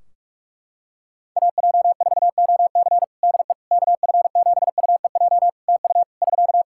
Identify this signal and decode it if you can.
Morse code, heard as a keyed tone, about 46 wpm, 700 Hz. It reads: AJ4OC DE KF7UEJ TU <SK>
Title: Practice QSO - Exchange 4 - Ending - 45wpm